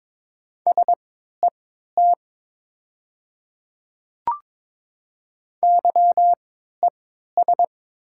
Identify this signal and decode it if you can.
Morse code, heard as a keyed tone, about 22 wpm, spaced as for 13 wpm, 700 Hz. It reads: SET YES
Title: 3x Character Spacing - Top 500 Words - 22wpm